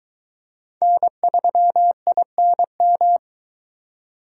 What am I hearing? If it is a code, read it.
Morse code, heard as a keyed tone, about 22 wpm, 700 Hz. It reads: N3INM